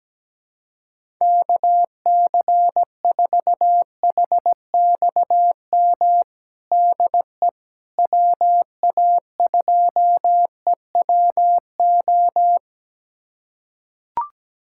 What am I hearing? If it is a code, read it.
Morse code, heard as a keyed tone, about 17 wpm, 700 Hz. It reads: KC4HXM DE WA2EWO
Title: USA Callsign DE Callsign 17wpm